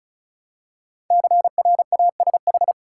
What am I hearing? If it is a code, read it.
Morse code, heard as a keyed tone, about 36 wpm, 700 Hz. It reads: CRASH
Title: Five Letter Words - 35wpm